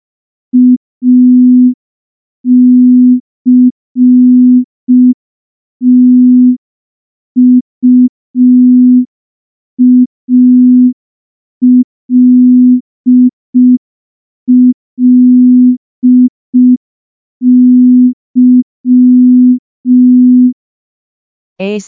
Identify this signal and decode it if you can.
Morse code, heard as a keyed tone, about 5 wpm, 250 Hz. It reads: ACTUALLY
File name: morse / morse-training-05WPM